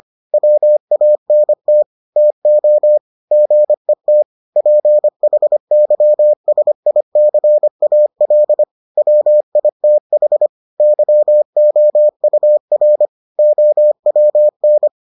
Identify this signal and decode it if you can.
Morse code, heard as a keyed tone, about 25 wpm, 600 Hz. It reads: WANT TO GET PHYSICAL WITH YOUR OWN